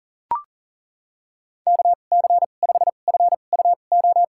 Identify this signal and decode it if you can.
Morse code, heard as a keyed tone, about 40 wpm, spaced as for 29 wpm, 700 Hz. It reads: KC5FUO
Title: USA Call Signs - 6 Characters - 2x Character Spacing - 40wpm